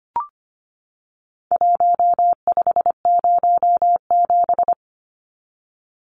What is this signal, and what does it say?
Morse code, heard as a keyed tone, about 25 wpm, 700 Hz. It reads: 1507
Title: Numbers - Contest Style - 4-Digit 25wpm